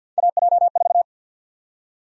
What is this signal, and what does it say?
Morse code, heard as a keyed tone, about 50 wpm, 700 Hz. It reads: AJ4
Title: USA Call Sign Prefixes - Repeated 2x - 50wpm